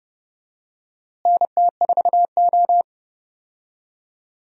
Morse code, heard as a keyed tone, about 30 wpm, 700 Hz. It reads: NT4O
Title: USA Call Signs - 4 Characters - Repeated 2x - 30wpm